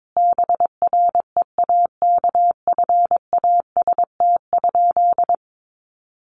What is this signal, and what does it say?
Morse code, heard as a keyed tone, about 22 wpm, 700 Hz. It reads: BREAKFAST?
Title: Sentences from Top 1500 Words 22wpm